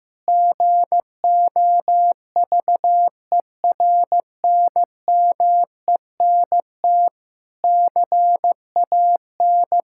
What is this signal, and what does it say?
Morse code, heard as a keyed tone, about 15 wpm, 700 Hz. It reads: GOVERNMENT CAN
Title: Sentences from Top 1000 Words - Review Part 5 - 15wpm